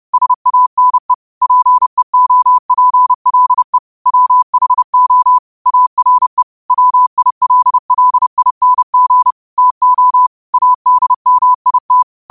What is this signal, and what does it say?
Morse code, heard as a keyed tone, about 30 wpm, 1000 Hz. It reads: SANE PEOPLE WHO ARE WILLING TO ADMIT